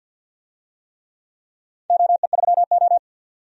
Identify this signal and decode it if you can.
Morse code, heard as a keyed tone, about 47 wpm, 700 Hz. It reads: OE3O
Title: DXCC-100 Call Signs - 4 Characters - 50wpm